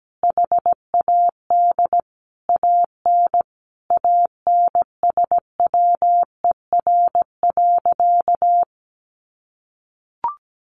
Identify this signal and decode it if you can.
Morse code, heard as a keyed tone, about 17 wpm, 700 Hz. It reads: HAD AN ANSWER.
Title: Sentences from Top 1200 Words 17wpm